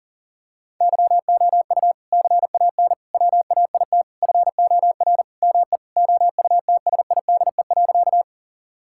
Morse code, heard as a keyed tone, about 37 wpm, 700 Hz. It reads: YOU CAN WAIT FOR ME OUTSIDE.
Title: Sentences from Top 1000 Words - Review Part 6 - 40wpm